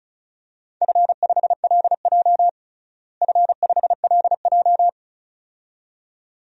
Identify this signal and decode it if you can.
Morse code, heard as a keyed tone, about 36 wpm, 700 Hz. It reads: F5LJ F5LJ
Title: DXCC-100 Call Signs - 4 Characters - Repeated 2x - 35wpm